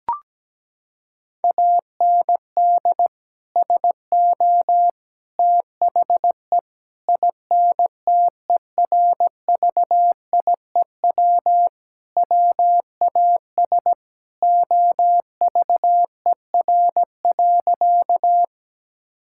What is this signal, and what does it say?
Morse code, heard as a keyed tone, about 17 wpm, 700 Hz. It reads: AND SO THE INTERVIEW WAS OVER.
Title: Sentences from Top 1600 Words 17wpm